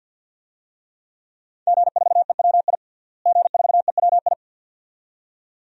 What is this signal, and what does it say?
Morse code, heard as a keyed tone, about 52 wpm, 700 Hz. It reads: G4EWI G4EWI
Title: DXCC-100 Call Signs - 5 Characters - Repeated 2x - 50wpm